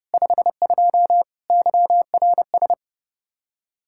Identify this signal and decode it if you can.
Morse code, heard as a keyed tone, about 40 wpm, 700 Hz. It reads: 52 YRS